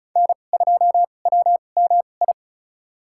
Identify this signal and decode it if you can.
Morse code, heard as a keyed tone, about 34 wpm, spaced as for 25 wpm, 700 Hz. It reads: N2WMI
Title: USA Call Signs - 5 Characters - 2x Character Spacing - 35wpm